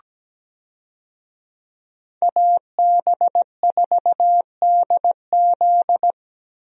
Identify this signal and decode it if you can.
Morse code, heard as a keyed tone, about 17 wpm, 700 Hz. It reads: AB4DZ
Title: USA Call Signs - 5 Characters - 17wpm